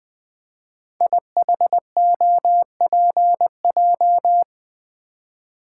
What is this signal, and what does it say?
Morse code, heard as a keyed tone, about 20 wpm, 700 Hz. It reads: IHOPJ
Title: Code Groups - 5 Letters - 20wpm